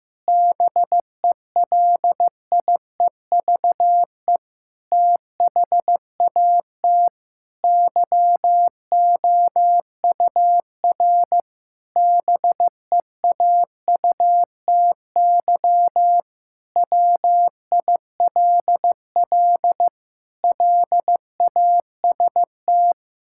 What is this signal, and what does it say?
Morse code, heard as a keyed tone, about 15 wpm, 700 Hz. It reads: BELIEVE THAT YOUR BEAUTY WILL LAST
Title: Sentences from Top 1600 Words 15wpm